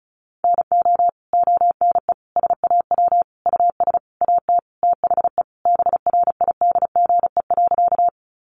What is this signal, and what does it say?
Morse code, heard as a keyed tone, about 34 wpm, 700 Hz. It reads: NO ONE SAW US AT THE BRIDGE.
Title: Sentences from Top 1200 Words 35wpm